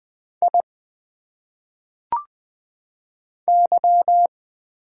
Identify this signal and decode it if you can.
Morse code, heard as a keyed tone, about 20 wpm, 700 Hz. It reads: I Y
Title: Single Letters - Phonetic Pronunciation 20wpm